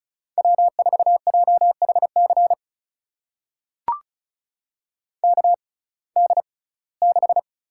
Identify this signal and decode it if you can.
Morse code, heard as a keyed tone, about 36 wpm, spaced as for 21 wpm, 700 Hz. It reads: W4JHC KD6
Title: USA Call Signs - 6x Character Spacing - 35wpm